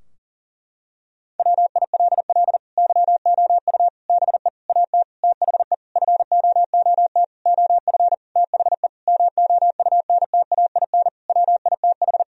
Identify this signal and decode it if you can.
Morse code, heard as a keyed tone, about 40 wpm, 700 Hz. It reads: WILL YOU BE AT THE FOOT OF THE MOUNTAIN WITH
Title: Sentences from Top 1000 Words - Review Part 3 - 40wpm